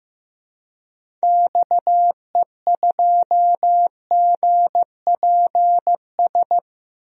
Morse code, heard as a keyed tone, about 15 wpm, 700 Hz. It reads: XE2GPS
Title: DXCC-100 Call Signs - 6 Characters - 15wpm